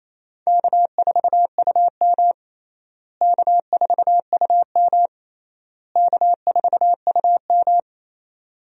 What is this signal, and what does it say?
Morse code, heard as a keyed tone, about 28 wpm, 700 Hz. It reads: K4UM K4UM K4UM
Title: USA Call Signs - 4 Characters - Repeated 3x - 28wpm